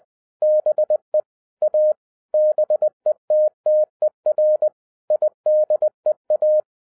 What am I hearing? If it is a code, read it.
Morse code, heard as a keyed tone, about 20 wpm, 610 Hz. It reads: BE A BETTER IDEA